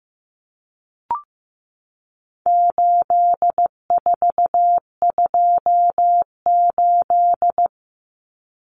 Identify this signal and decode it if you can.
Morse code, heard as a keyed tone, about 15 wpm, 700 Hz. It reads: 8428
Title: Numbers - 4-Digit 15wpm